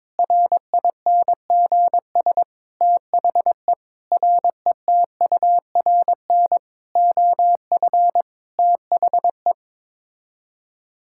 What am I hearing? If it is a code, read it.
Morse code, heard as a keyed tone, about 22 wpm, 700 Hz. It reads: RINGS THE RETURN OF THE
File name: Sets of 9 Words - 22wpm